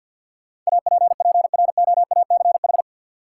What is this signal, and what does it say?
Morse code, heard as a keyed tone, about 50 wpm, 700 Hz. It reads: APPROACH
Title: Sentences from Top 1000 Words - Review Part 6 - New Word Drill - 50wpm